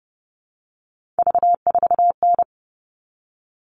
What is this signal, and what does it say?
Morse code, heard as a keyed tone, about 30 wpm, 700 Hz. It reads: V4N
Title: All Cut Numbers - 3-Digit 30wpm